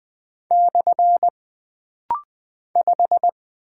Morse code, heard as a keyed tone, about 20 wpm, 700 Hz. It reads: / 5
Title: Single Character (Letter:Number:Punctuation:Prosign) - Rapid-Fire - 20wpm